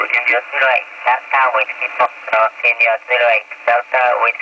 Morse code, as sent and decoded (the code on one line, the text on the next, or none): none